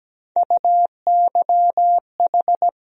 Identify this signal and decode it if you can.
Morse code, heard as a keyed tone, about 17 wpm, 700 Hz. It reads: UYH
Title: USA Call Sign Postfixes - 3 Letters 17wpm